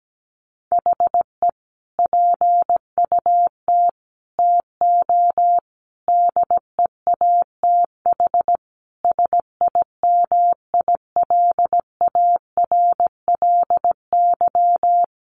Morse code, heard as a keyed tone, about 17 wpm, 700 Hz. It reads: HE PUT TO DEATH SIMILARLY
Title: Sentences from Top 1500 Words 17wpm